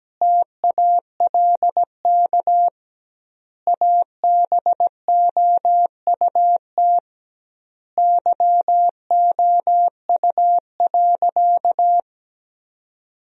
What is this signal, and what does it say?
Morse code, heard as a keyed tone, about 17 wpm, 700 Hz. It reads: TALK ABOUT YOU.
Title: 2x Word-Spacing - Sentences from Top 400 Words - 17wpm